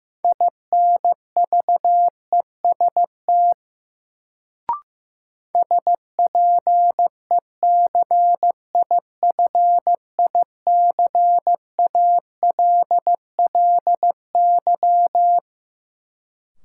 Morse code, heard as a keyed tone, about 15 wpm, 700 Hz. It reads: INVEST SPECIFICALLY
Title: Sentences from Top 1100 Words - New Word Drill - 15wpm